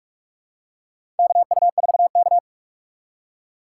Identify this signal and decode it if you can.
Morse code, heard as a keyed tone, about 44 wpm, 690 Hz. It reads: KU4K
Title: USA Call Signs - 4 Characters - 2x Character Spacing - 45wpm